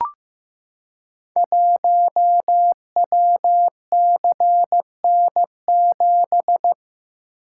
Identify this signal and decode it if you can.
Morse code, heard as a keyed tone, about 15 wpm, 700 Hz. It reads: 1WCN7
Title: Code Groups - 5 Letter:Numbers - 15wpm